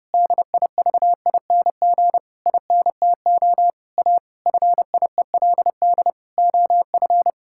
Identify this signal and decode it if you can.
Morse code, heard as a keyed tone, about 29 wpm, 700 Hz. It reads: DIVING INTO A FIELD OF
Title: Pangram Challenge v2 - 30wpm